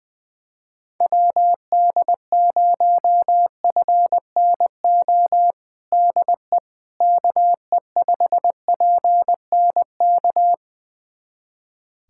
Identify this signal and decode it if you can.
Morse code, heard as a keyed tone, about 20 wpm, 700 Hz. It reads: WD0FNO DE KE5PNK
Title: USA Callsign DE Callsign - 6 Characters - 20wpm